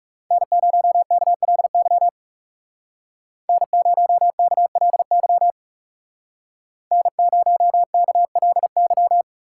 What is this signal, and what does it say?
Morse code, heard as a keyed tone, about 40 wpm, 700 Hz. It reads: N0KLY N0KLY N0KLY